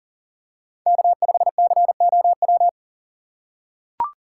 Morse code, heard as a keyed tone, about 40 wpm, 700 Hz. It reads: K5COW
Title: USA Call Signs - 5 Characters - 2x Character Spacing - 40wpm